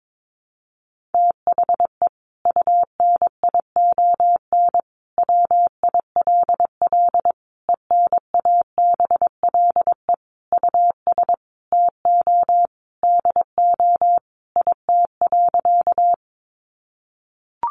Morse code, heard as a keyed tone, about 22 wpm, 700 Hz. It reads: THE UNION WILL ENABLE US TO DO IT.
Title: Sentences from Top 1500 Words 22wpm